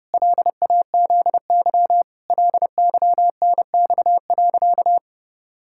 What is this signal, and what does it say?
Morse code, heard as a keyed tone, about 30 wpm, 700 Hz. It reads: LAZY LYNX.
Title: Pangram Challenge v2 - 30wpm